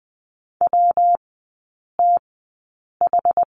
Morse code, heard as a keyed tone, about 20 wpm, spaced as for 8 wpm, 700 Hz. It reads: WTH
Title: 114 Lesson 19e - Letter w - Three Characters 20wpm